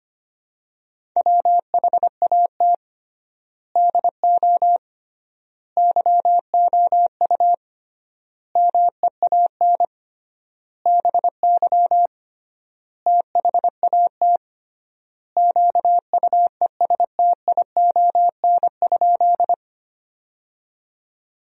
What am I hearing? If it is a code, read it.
Morse code, heard as a keyed tone, about 25 wpm, 700 Hz. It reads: WHAT DO YOU MEAN BY THAT QUESTION?
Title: 3x Word-Spacing - Sentences from Top 400 Words - 25wpm